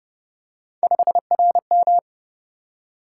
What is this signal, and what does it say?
Morse code, heard as a keyed tone, about 26 wpm, 700 Hz. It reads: 5RM